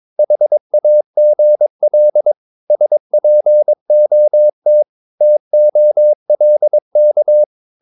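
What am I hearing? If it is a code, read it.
Morse code, heard as a keyed tone, about 22 wpm, 600 Hz. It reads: HAGL SPOT TOLK
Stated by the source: Morse practice of short Danish words at 22 wpm, from CW med Gnister.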